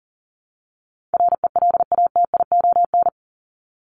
Morse code, heard as a keyed tone, about 40 wpm, 700 Hz. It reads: RELATION
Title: ES:S Suffix - Top 100 Words with Base Words 40wpm